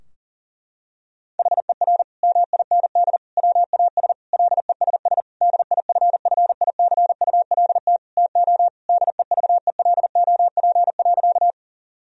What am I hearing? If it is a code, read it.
Morse code, heard as a keyed tone, about 40 wpm, 700 Hz. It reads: HER MIND WAS LESS DIFFICULT TO DEVELOP.
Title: Sentences from Top 1000 Words - Review Part 2 - 40wpm